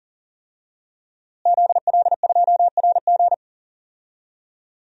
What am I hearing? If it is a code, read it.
Morse code, heard as a keyed tone, about 41 wpm, 700 Hz. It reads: ZL2RG
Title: DXCC-100 Call Signs - 5 Characters - Repeated 3x - 40wpm